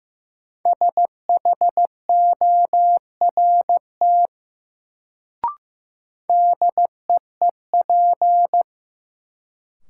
Morse code, heard as a keyed tone, about 16 wpm, 700 Hz. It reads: SHORT DEEP
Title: Sets of 5 Words Encore - New Word Drill - 15wpm